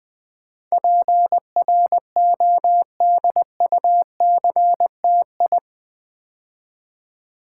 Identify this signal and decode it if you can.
Morse code, heard as a keyed tone, about 20 wpm, 700 Hz. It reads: PRODUCTI
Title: Build A Word - 4 Letters At A Time - 12-Letter Words - 20wpm